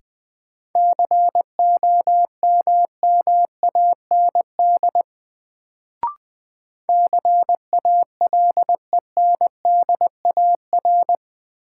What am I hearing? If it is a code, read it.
Morse code, heard as a keyed tone, about 20 wpm, 700 Hz. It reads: COMMAND CALENDAR
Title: Sentences from Top 1600 Words - New Word Drill - 20wpm